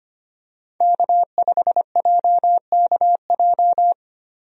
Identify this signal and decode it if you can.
Morse code, heard as a keyed tone, about 25 wpm, 700 Hz. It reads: K5JKJ
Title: USA Call Signs - 5 Characters - Repeated 3x - 25wpm